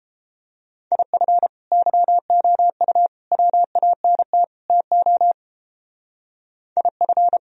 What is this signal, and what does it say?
Morse code, heard as a keyed tone, about 31 wpm, 700 Hz. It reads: IF YOU WANT TO IF